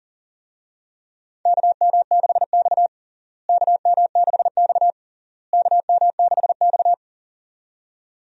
Morse code, heard as a keyed tone, about 40 wpm, 700 Hz. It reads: KM6X KM6X KM6X
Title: USA Call Signs - 4 Characters - Repeated 3x - 40wpm